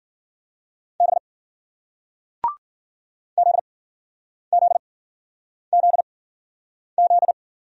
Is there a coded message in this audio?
Yes: Morse code from 1.0 to 7.3 s, heard as a keyed tone, about 53 wpm, spaced as for 9 wpm, 700 Hz. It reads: B ZZZZ